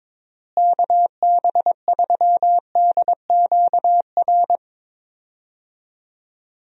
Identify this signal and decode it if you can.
Morse code, heard as a keyed tone, about 22 wpm, 700 Hz. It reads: KB3DQR
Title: USA Call Signs - 6 Characters - Repeated 3x - 22wpm